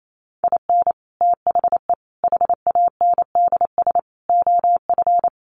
Morse code, heard as a keyed tone, about 28 wpm, 700 Hz. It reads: IN THE HANDS OF
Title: Sets of 5 Words 28wpm